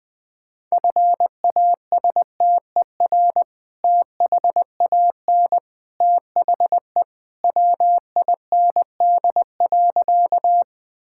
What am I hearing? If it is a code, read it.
Morse code, heard as a keyed tone, about 20 wpm, 700 Hz. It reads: FASTER THAN THE WIND.
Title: Sentences with 2-Word Phrases - Spoken Beginning - 20wpm